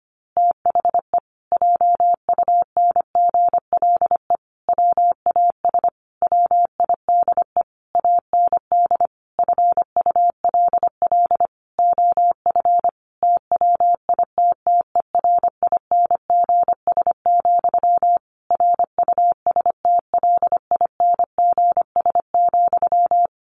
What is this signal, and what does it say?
Morse code, heard as a keyed tone, about 25 wpm, 700 Hz. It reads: THE JUNGLE WAS WIDE AND FULL OF TWITTERINGS, RUSTLINGS,
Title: A Sound Of Thunder 25wpm